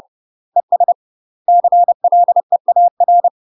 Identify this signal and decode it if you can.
Morse code, heard as a keyed tone, about 30 wpm, 700 Hz. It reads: ES CLEAR
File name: UK MASTER QSOs TRAINING/30WPM